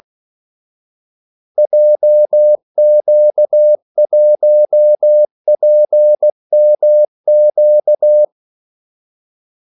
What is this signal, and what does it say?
Morse code, heard as a keyed tone, about 16 wpm, 600 Hz. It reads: JQ1PMQ
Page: Kaldesignaler 16 wpm